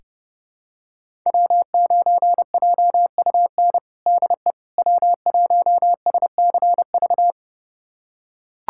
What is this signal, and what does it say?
Morse code, heard as a keyed tone, about 30 wpm, 700 Hz. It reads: W9JUN DE W1SCV
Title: USA Callsign DE Callsign - 5 Characters - 30wpm